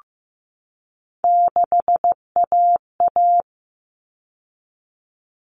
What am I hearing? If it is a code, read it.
Morse code, heard as a keyed tone, about 15 wpm, 700 Hz. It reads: 6AA
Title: Common Cut Numbers - 3-Digit 15wpm